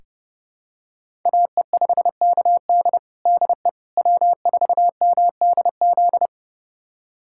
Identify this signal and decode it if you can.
Morse code, heard as a keyed tone, about 30 wpm, 700 Hz. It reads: AE5KD DE W4MDZ
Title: USA Callsign DE Callsign - 5 Characters - 30wpm